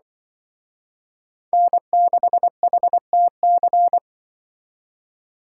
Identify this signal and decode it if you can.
Morse code, heard as a keyed tone, about 24 wpm, 700 Hz. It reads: N6HTC
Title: Speed Racing - USA 1x3 Call Signs 20wpm